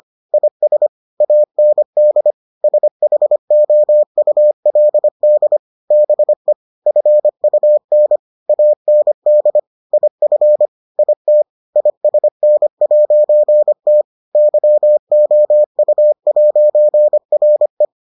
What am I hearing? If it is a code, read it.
Morse code, heard as a keyed tone, about 25 wpm, 600 Hz. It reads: IS AND SHOULD BE FUN AND IF IT ISN'T YOU'RE